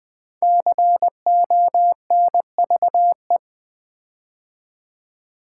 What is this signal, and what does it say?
Morse code, heard as a keyed tone, about 20 wpm, 700 Hz. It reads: CONVE
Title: Build A Word - 5 Letters At A Time - 10-Letter Words - 20wpm